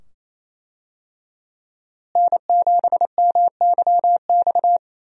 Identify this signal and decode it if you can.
Morse code, heard as a keyed tone, about 28 wpm, 700 Hz. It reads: N7MYX
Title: USA Call Signs - 5 Characters - Repeated 2x - 28wpm